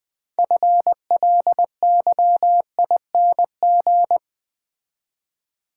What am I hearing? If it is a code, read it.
Morse code, heard as a keyed tone, about 20 wpm, 700 Hz. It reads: FLYING
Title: Build A Word - 2 Letters At A Time - 6-Letter Words - 20wpm